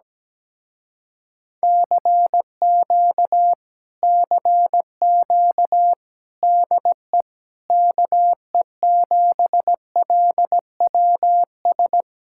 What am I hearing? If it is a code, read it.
Morse code, heard as a keyed tone, about 17 wpm, 700 Hz. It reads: CQ CQ DE KE7LWS
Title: Speed-Racing Common QSO Phrases Encore - 17wpm